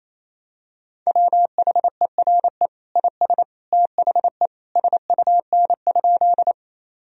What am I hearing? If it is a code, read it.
Morse code, heard as a keyed tone, about 28 wpm, 700 Hz. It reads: WHERE IS THE SUN?
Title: Sentences from Top 1000 Words - Review Part 1 - 28wpm